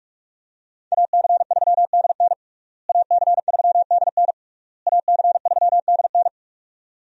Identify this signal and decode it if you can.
Morse code, heard as a keyed tone, about 42 wpm, 700 Hz. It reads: AC3DN AC3DN AC3DN
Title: USA Call Signs - 5 Characters - Repeated 3x - 45wpm